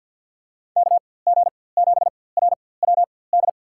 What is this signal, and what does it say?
Morse code, heard as a keyed tone, about 53 wpm, spaced as for 24 wpm, 700 Hz. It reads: KG7RWD
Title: USA Call Signs - 6 Characters - 4x Character Spacing - 50wpm